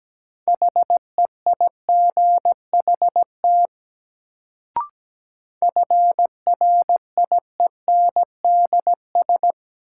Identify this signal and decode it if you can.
Morse code, heard as a keyed tone, about 17 wpm, 700 Hz. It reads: HEIGHT FRIENDS
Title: Sentences from Top 1600 Words - New Word Drill Alt - 17wpm